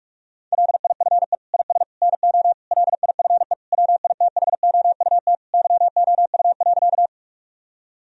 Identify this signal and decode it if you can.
Morse code, heard as a keyed tone, about 47 wpm, 700 Hz. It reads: LIFE IS NO LIFE WITHOUT YOU.
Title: Sentences from Top 1000 Words - Review Part 3 - 45wpm